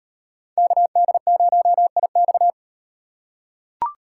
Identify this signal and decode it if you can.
Morse code, heard as a keyed tone, about 38 wpm, 700 Hz. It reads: KD0IX